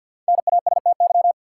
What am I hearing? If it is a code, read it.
Morse code, heard as a keyed tone, about 50 wpm, 700 Hz. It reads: NASTY
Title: Five Letter Words - 50wpm